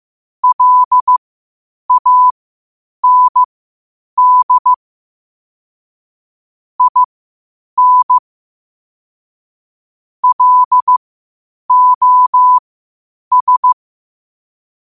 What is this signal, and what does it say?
Morse code, heard as a keyed tone, about 15 wpm, spaced as for 8 wpm, 1000 Hz. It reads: LAND IN LOS
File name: qotdfiles / 8wpm